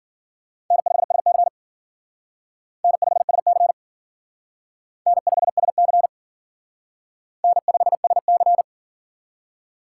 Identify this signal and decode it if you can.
Morse code, heard as a keyed tone, about 50 wpm, 700 Hz. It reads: N5SC N5SC N5SC N5SC
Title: Speed Racing - USA 1x2 Call Signs 40wpm